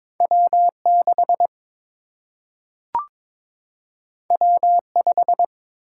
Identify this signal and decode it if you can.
Morse code, heard as a keyed tone, about 22 wpm, 700 Hz. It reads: W 6 W 5
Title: Single-Letter Single-Number 22wpm